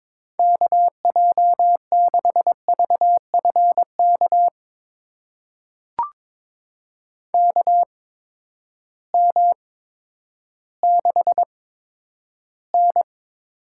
Morse code, heard as a keyed tone, about 22 wpm, spaced as for 11 wpm, 700 Hz. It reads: KJ6VFK KM6N
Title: USA Call Signs - 6 Characters - 8x Character Spacing - 22wpm